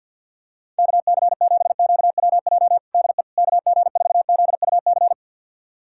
Morse code, heard as a keyed tone, about 51 wpm, 700 Hz. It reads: KC7QWJ DE KG4ZUC
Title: USA Callsign DE Callsign - 6 Characters - 50wpm